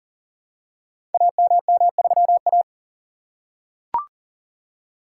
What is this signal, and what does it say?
Morse code, heard as a keyed tone, about 40 wpm, 700 Hz. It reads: AMM3A
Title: Code Groups - 5 Letter:Numbers - 40wpm